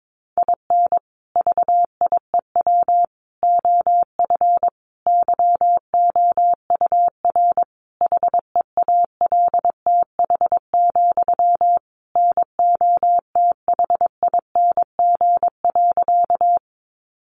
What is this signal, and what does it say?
Morse code, heard as a keyed tone, about 22 wpm, 700 Hz. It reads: IN VIEW OF YOUR HEALTH, NOTHING.
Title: Sentences from Top 900 Words 22wpm